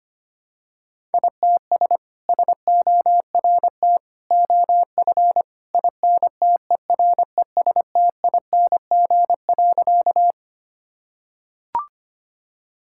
Morse code, heard as a keyed tone, about 25 wpm, 700 Hz. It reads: ITS SORT OF INTERESTING.
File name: practice / 25wpm / Sentences with 2-Word Phrases - 25wpm